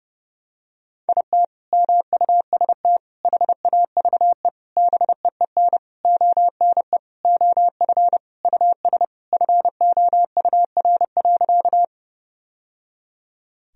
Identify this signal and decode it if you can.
Morse code, heard as a keyed tone, about 32 wpm, 700 Hz. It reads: IT MUST HAVE BEEN ONE OF US FOUR.
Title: Sentences from Top 1000 Words - Review Part 1 - 30wpm